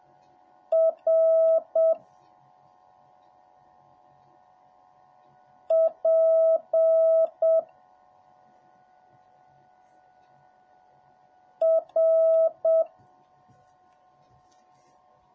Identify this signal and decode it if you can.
Morse code, heard as a keyed tone, about 7 wpm, 650 Hz. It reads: R P R